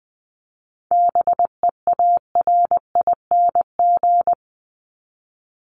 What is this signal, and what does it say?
Morse code, heard as a keyed tone, about 20 wpm, 700 Hz. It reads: BEARING
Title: ING Suffix - Top 100 Words with Base Words 20wpm